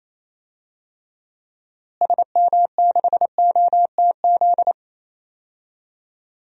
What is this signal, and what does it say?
Morse code, heard as a keyed tone, about 28 wpm, 700 Hz. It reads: SM6OTZ
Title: DXCC-100 Call Signs - 6 Characters - Repeated 2x - 28wpm